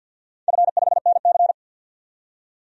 Morse code, heard as a keyed tone, about 53 wpm, 700 Hz. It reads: F5NC
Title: DXCC-100 Call Signs - 4 Characters - 50wpm